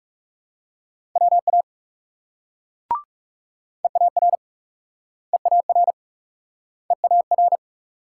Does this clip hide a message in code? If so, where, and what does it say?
Yes: Morse code from 1.2 to 7.6 s, heard as a keyed tone, about 40 wpm, 700 Hz. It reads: WA EAR EAR EAR